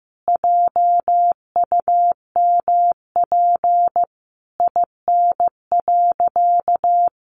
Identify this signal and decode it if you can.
Morse code, heard as a keyed tone, about 15 wpm, 700 Hz. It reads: JUMP IN.
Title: Sentences from Top 1300 Words 15wpm